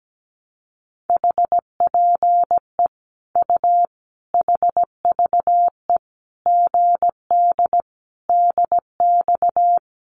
Morse code, heard as a keyed tone, about 17 wpm, 700 Hz. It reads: HPE U HVE GD DX
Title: Common QSO Phrases 17wpm